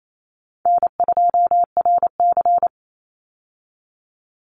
Morse code, heard as a keyed tone, about 28 wpm, 700 Hz. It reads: N2RC
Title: Easy Call Signs - 28wpm